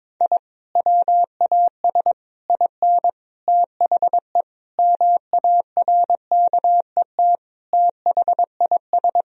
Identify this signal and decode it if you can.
Morse code, heard as a keyed tone, about 22 wpm, 700 Hz. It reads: I WAS IN THE MARKET THIS
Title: Sentences from Top 1000 Words - Review Part 3 - 22wpm